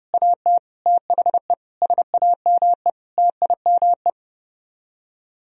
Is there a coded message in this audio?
Yes: Morse code from 0.1 to 4.1 s, heard as a keyed tone, about 30 wpm, 700 Hz. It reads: AT THE SAME TIME